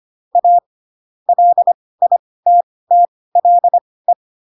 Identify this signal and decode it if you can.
Morse code, heard as a keyed tone, about 25 wpm, spaced as for 18 wpm, 700 Hz. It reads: A LITTLE